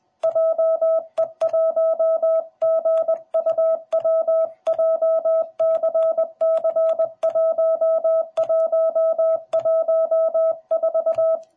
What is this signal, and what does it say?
Morse code, heard as a keyed tone, about 20 wpm, 650 Hz. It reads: JE1GUW JCC1114